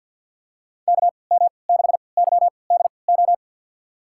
Morse code, heard as a keyed tone, about 52 wpm, spaced as for 29 wpm, 700 Hz. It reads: KM6YDO
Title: USA Call Signs - 6 Characters - 3x Character Spacing - 50wpm